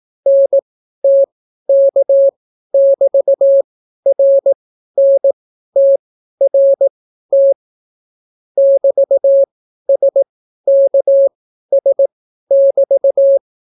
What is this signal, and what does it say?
Morse code, heard as a keyed tone, about 18 wpm, spaced as for 12 wpm, 550 Hz. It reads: NTK<BT>RNTRT <BT>SKS<BT>